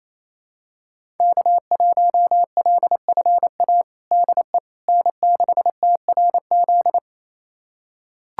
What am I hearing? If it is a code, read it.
Morse code, heard as a keyed tone, about 28 wpm, 700 Hz. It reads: K1LFA DE N6TRZ
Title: USA Callsign DE Callsign - 5 Characters - 28wpm